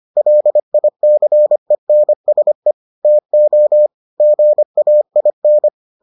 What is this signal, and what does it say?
Morse code, heard as a keyed tone, about 25 wpm, 600 Hz. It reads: LICENSE TO GAIN